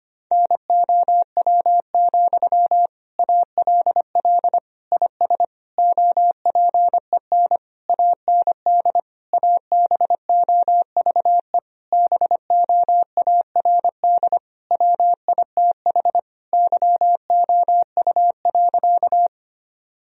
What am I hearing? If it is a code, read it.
Morse code, heard as a keyed tone, about 25 wpm, 700 Hz. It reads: NOW, ALL IS OPEN AND ABOVE BOARD WITH YOU.
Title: Sentences from Top 1000 Words - Review Part 4 - 25wpm